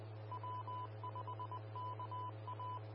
Morse code, heard as a keyed tone, about 20 wpm, 1000 Hz. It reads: W5KA
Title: Austin, TX Amateur Repeaters